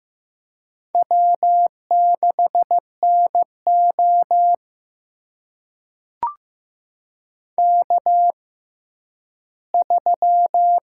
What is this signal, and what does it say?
Morse code, heard as a keyed tone, about 15 wpm, spaced as for 10 wpm, 700 Hz. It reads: W6NO K3
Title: USA Call Signs - 6x Character Spacing - 15wpm